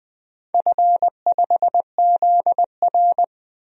Morse code, heard as a keyed tone, about 20 wpm, 700 Hz. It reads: F5ZR
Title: DXCC-100 Call Signs - 4 Characters - 20wpm